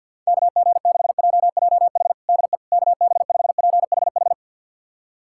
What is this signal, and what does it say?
Morse code, heard as a keyed tone, about 48 wpm, 700 Hz. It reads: KG6JJH DE KB5PHH
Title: USA Callsign DE Callsign 50wpm